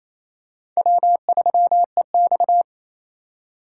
Morse code, heard as a keyed tone, about 28 wpm, 700 Hz. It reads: W3EX
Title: USA Call Signs - 3x Character Spacing - 28wpm